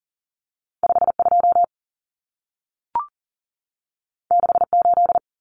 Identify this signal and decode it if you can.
Morse code, heard as a keyed tone, about 40 wpm, 700 Hz. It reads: 52 68
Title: Common Cut Numbers - 2-Digit 40wpm